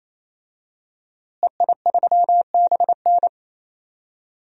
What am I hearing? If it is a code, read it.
Morse code, heard as a keyed tone, about 28 wpm, 700 Hz. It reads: EI3BN
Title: DXCC-100 Call Signs - 5 Characters - Repeated 3x - 28wpm